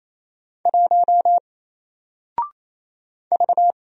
Morse code, heard as a keyed tone, about 28 wpm, 700 Hz. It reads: 1 V
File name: Single-Letter-Number-28wpm